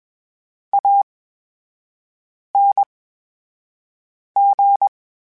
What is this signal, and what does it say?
Morse code, heard as a keyed tone, about 21 wpm, spaced as for 5 wpm, 800 Hz. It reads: ANG
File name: Special INGs-mp3s / Special INGs-mp3s / VerySlowToFaster Ing @20